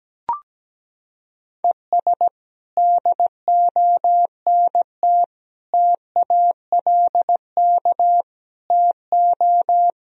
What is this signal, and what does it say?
Morse code, heard as a keyed tone, about 17 wpm, 700 Hz. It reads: ES DONT TALK TO
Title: Sentences from Top 400 Words and Abbreviations 17wpm